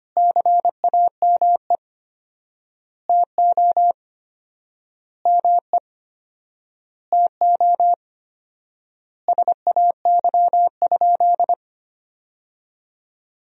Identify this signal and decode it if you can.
Morse code, heard as a keyed tone, about 25 wpm, 700 Hz. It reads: CAME TO ME TO SAY?
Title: 4x Word-Spacing - Sentences from Top 200 Words - 25wpm